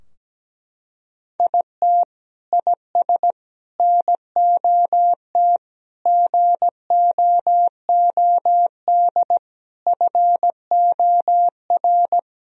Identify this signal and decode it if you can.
Morse code, heard as a keyed tone, about 17 wpm, 700 Hz. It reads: IT IS NOT GOOD FOR
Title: Sentences from Top 1000 Words - Review Part 2 - 17wpm